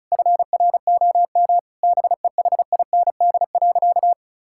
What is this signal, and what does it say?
Morse code, heard as a keyed tone, about 34 wpm, 700 Hz. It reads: FROM BEHIND.
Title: Sentences from Top 1600 Words 35wpm